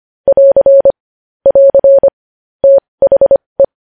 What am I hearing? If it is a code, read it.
Morse code, heard as a keyed tone, about 25 wpm, spaced as for 20 wpm, 550 Hz. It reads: <AR> <AR> THE
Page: news headlines in morse code at 20 wpm